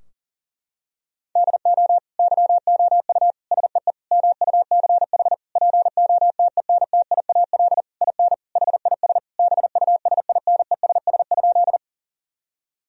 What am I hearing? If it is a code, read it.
Morse code, heard as a keyed tone, about 37 wpm, 700 Hz. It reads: DO YOU SEE MUCH POTENTIAL IN HIS BUSINESS?
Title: Sentences from Top 1000 Words - Review Part 4 - 40wpm